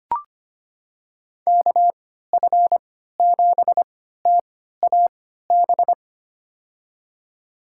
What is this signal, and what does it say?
Morse code, heard as a keyed tone, about 25 wpm, spaced as for 14 wpm, 700 Hz. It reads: KF7TAB
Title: USA Call Signs - 6 Characters - 3x Character Spacing - 25wpm